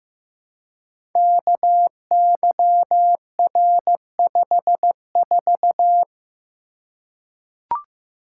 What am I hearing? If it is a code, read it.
Morse code, heard as a keyed tone, about 15 wpm, 700 Hz. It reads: KYR54
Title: Code Groups - 5 Letter:Numbers - 15wpm